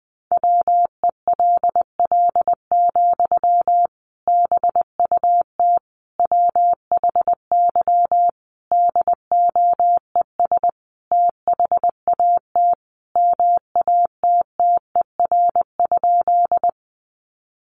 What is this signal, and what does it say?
Morse code, heard as a keyed tone, about 20 wpm, 700 Hz. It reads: WELL, BUT WHY DOES THAT MATTER?
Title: Sentences from Top 700 Words 20wpm